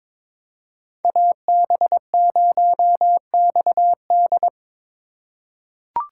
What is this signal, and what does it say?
Morse code, heard as a keyed tone, about 22 wpm, 700 Hz. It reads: AB0XD
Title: USA Call Signs - 5 Characters - 6x Character Spacing - 22wpm